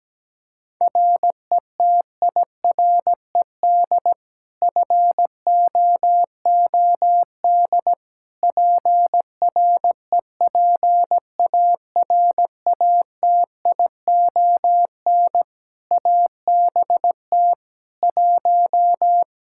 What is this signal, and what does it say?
Morse code, heard as a keyed tone, about 17 wpm, 700 Hz. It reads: RETIRED FOOD PREPARATION ABT 1
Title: Practice QSO - Exchange 3 - More Info - 17wpm